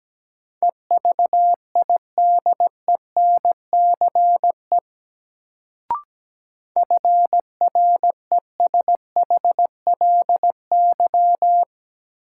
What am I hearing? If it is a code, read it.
Morse code, heard as a keyed tone, about 17 wpm, 700 Hz. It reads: EVIDENCE FRESHLY
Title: Sets of 5 Words Encore - New Word Drill - 17wpm